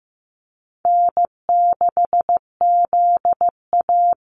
Morse code, heard as a keyed tone, about 15 wpm, 700 Hz. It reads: N6ZA
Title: NA SOTA Activator Callsigns 15wpm